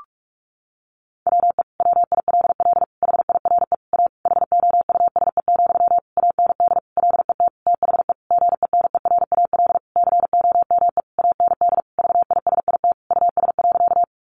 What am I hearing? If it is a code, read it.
Morse code, heard as a keyed tone, about 44 wpm, 700 Hz. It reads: WE WILL HIRE A HOUSE, AND LET THE GENERAL COME AND VISIT US.
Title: Sentences from Top 1400 Words 45wpm